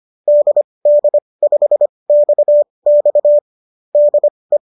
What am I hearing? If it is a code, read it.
Morse code, heard as a keyed tone, about 25 wpm, spaced as for 20 wpm, 590 Hz. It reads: DD5XX DE